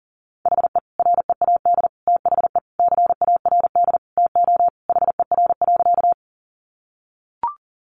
Morse code, heard as a keyed tone, about 40 wpm, 700 Hz. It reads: HE READ THE CARD TO HER.
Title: Sentences from Top 600 Words 40wpm